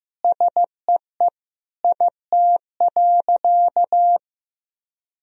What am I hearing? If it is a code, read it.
Morse code, heard as a keyed tone, about 16 wpm, 700 Hz. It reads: SEE IT.
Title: Sentences from Top 1000 Words - Review Part 4 - 15wpm